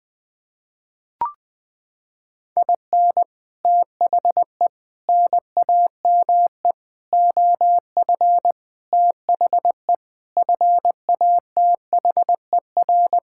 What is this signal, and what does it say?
Morse code, heard as a keyed tone, about 20 wpm, 700 Hz. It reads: IN THE NAME OF THE FATHER
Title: Sets of 9 Words - 20wpm